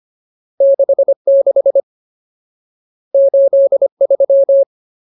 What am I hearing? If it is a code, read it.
Morse code, heard as a keyed tone, about 25 wpm, 550 Hz. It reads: 66 83
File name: Ultimate Word List / 25WPM Ultimate